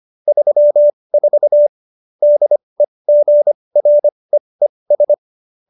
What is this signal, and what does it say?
Morse code, heard as a keyed tone, about 25 wpm, spaced as for 20 wpm, 590 Hz. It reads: 34 DEGREES